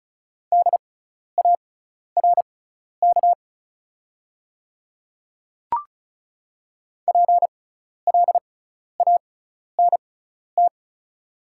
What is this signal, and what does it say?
Morse code, heard as a keyed tone, about 34 wpm, spaced as for 12 wpm, 700 Hz. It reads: DARK PLANT
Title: 6x Character Spacing - Top 500 Words - 35wpm